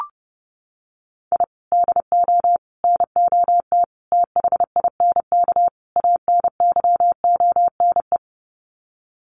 Morse code, heard as a keyed tone, about 30 wpm, 700 Hz. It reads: I DO NOT THINK ANYONE
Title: Sets of 5 Words Encore 30wpm